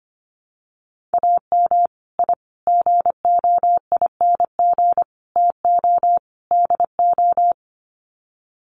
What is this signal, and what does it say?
Morse code, heard as a keyed tone, about 25 wpm, 700 Hz. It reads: AM I GOING TO DO
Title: Sets of 5 Words Encore 25wpm